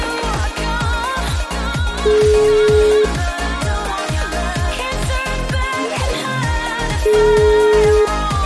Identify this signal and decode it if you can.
Ringback tone (Europe, 425 Hz), with noise.